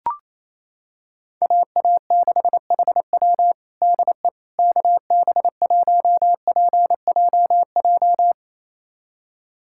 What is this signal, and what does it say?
Morse code, heard as a keyed tone, about 28 wpm, 700 Hz. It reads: AA6HW DE KB1PJJ
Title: USA Callsign DE Callsign 28wpm